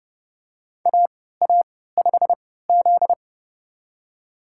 Morse code, heard as a keyed tone, about 30 wpm, spaced as for 17 wpm, 700 Hz. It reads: AA5Z
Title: USA Call Signs - 4 Characters - 3x Character Spacing - 30wpm